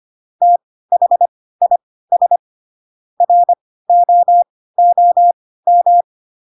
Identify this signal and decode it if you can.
Morse code, heard as a keyed tone, about 25 wpm, spaced as for 16 wpm, 700 Hz. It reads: THIS ROOM
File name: Pinocchio / Pinocchio ch4 10 25 ik1hge